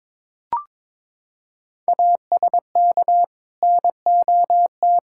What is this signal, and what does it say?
Morse code, heard as a keyed tone, about 22 wpm, 700 Hz. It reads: ASK NOT
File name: practice / 22wpm / Sets of 8 Words - 22wpm